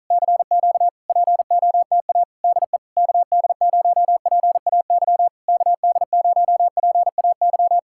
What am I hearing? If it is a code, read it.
Morse code, heard as a keyed tone, about 46 wpm, 700 Hz. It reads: CQ POTA DE KD0PAY KD0PAY